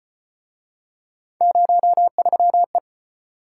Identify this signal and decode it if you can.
Morse code, heard as a keyed tone, about 35 wpm, 700 Hz. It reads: 03E